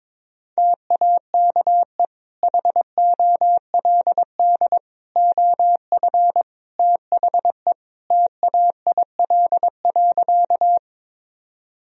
Morse code, heard as a keyed tone, about 22 wpm, 700 Hz. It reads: TAKE HOLD OF THE TAIL.
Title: Sentences from Top 1000 Words - Review Part 3 - 22wpm